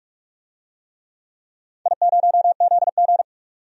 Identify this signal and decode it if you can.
Morse code, heard as a keyed tone, about 46 wpm, 700 Hz. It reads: I0ZG
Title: DXCC-100 Call Signs - 4 Characters - 45wpm